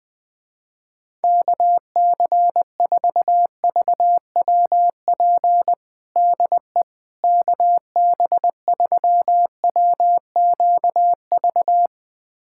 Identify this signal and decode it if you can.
Morse code, heard as a keyed tone, about 20 wpm, 700 Hz. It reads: KC4VWP DE KB3WQV
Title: USA Callsign DE Callsign 20wpm